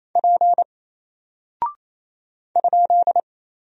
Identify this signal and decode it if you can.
Morse code, heard as a keyed tone, about 28 wpm, 700 Hz. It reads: P ?
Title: Single Character (Letter:Number:Punctuation:Prosign) Alt 28wpm